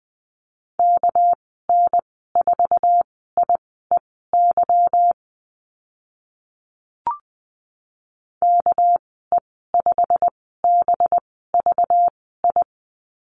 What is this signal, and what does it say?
Morse code, heard as a keyed tone, about 20 wpm, spaced as for 15 wpm, 700 Hz. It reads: KN4IEY KE5BVI
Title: 200 Lesson 32g - Letter k - Call Signs 20wpm